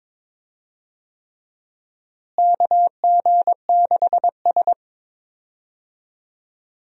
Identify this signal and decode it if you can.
Morse code, heard as a keyed tone, about 22 wpm, 700 Hz. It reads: KG6S
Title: USA Call Signs - 4 Characters - Repeated 3x - 22wpm